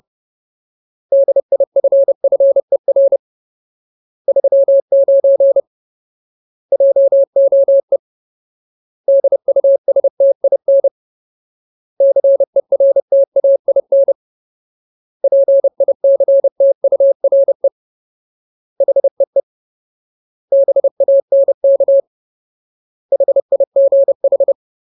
Morse code, heard as a keyed tone, about 30 wpm, 550 Hz. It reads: DIFFER 39 JOE DUSTIN CERTAIN PICTURE HEE BANK HIGH
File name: Ultimate Word List / 30WPM Ultimate